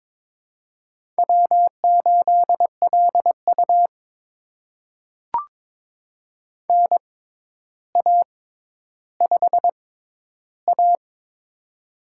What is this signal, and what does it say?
Morse code, heard as a keyed tone, about 22 wpm, spaced as for 11 wpm, 700 Hz. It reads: W8LU NA5A
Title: USA Call Signs - 4 Characters - 6x Character Spacing - 22wpm